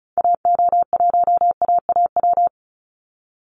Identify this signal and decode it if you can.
Morse code, heard as a keyed tone, about 34 wpm, 700 Hz. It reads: AO1AAW
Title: Call Signs - 35wpm